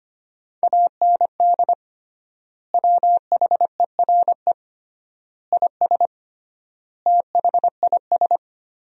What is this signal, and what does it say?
Morse code, heard as a keyed tone, about 25 wpm, 700 Hz. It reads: AND WHERE IS THIS
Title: 3x Word-Spacing - Sentences from Top 200 Words - 25wpm